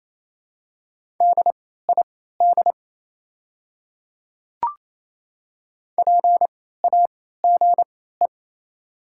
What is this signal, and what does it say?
Morse code, heard as a keyed tone, about 28 wpm, spaced as for 16 wpm, 700 Hz. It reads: DID PAGE
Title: 3x Character Spacing - Top 500 Words - 28wpm